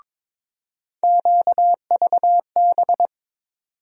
Morse code, heard as a keyed tone, about 22 wpm, 700 Hz. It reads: QVB
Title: USA Call Sign Postfixes - 3 Letters 22wpm